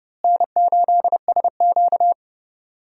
Morse code, heard as a keyed tone, about 31 wpm, 700 Hz. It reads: N8SQ
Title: USA Call Signs - 4 Characters - Repeated 2x - 30wpm